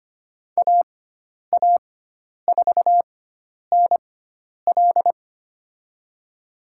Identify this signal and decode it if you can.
Morse code, heard as a keyed tone, about 25 wpm, spaced as for 10 wpm, 700 Hz. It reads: AA4NL
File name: Lesson 16g - Letter c - Call Signs 25wpm